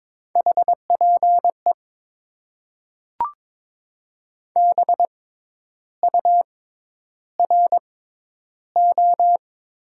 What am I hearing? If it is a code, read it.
Morse code, heard as a keyed tone, about 21 wpm, spaced as for 10 wpm, 700 Hz. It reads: HPE BURO
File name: Top 90 QSO Elements - 6x Character Spacing - 22wpm